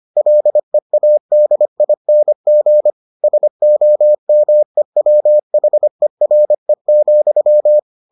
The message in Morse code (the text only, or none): LEADING SOMEWHERE,